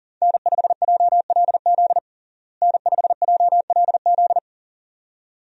Morse code, heard as a keyed tone, about 40 wpm, 700 Hz. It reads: N5JLZ N5JLZ
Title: USA Call Signs - 5 Characters - Repeated 2x - 40wpm